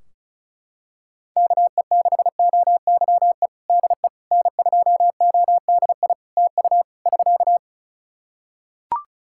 Morse code, heard as a keyed tone, about 36 wpm, 700 Hz. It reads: KE6OYE DE N2ODI TU <SK>
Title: K1USN SST QSO Practice 35wpm